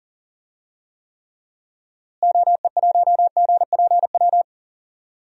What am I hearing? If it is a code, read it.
Morse code, heard as a keyed tone, about 39 wpm, 700 Hz. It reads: OE1GPW
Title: DXCC-100 Call Signs - 6 Characters - Repeated 3x - 40wpm